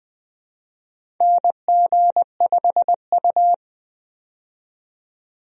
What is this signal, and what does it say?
Morse code, heard as a keyed tone, about 20 wpm, 700 Hz. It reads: NG5U
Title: USA Call Signs - 4 Characters - 20wpm